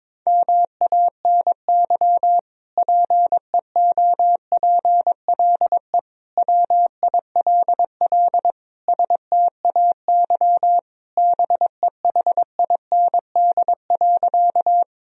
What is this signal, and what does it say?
Morse code, heard as a keyed tone, about 21 wpm, 700 Hz. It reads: MANY PEOPLE WILL STAY BEHIND.
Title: Sentences from Top 1000 Words - Review Part 1 - 22wpm